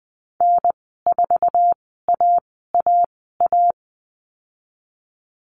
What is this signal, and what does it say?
Morse code, heard as a keyed tone, about 20 wpm, spaced as for 15 wpm, 700 Hz. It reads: N4AAA
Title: 055 Lesson 10g - Letter r - Call Signs 20wpm